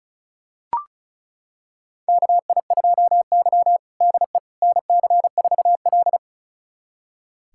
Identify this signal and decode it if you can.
Morse code, heard as a keyed tone, about 36 wpm, 700 Hz. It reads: KI2Y DE NC4L
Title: USA Callsign DE Callsign - 4 Characters - 35wpm